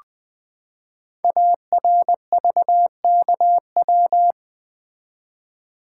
Morse code, heard as a keyed tone, about 20 wpm, 700 Hz. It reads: ARVKW
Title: Code Groups - 5 Letters - 20wpm